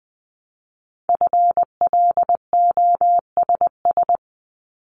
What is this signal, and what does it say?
Morse code, heard as a keyed tone, about 20 wpm, 700 Hz. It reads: FLOSS
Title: Multi-letter Phonemes Repeated 3x 20wpm